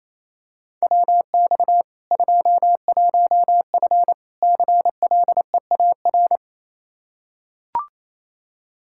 Morse code, heard as a keyed tone, about 28 wpm, 700 Hz. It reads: WX 21F CLEAR
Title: K1USN SST QSO Practice 28wpm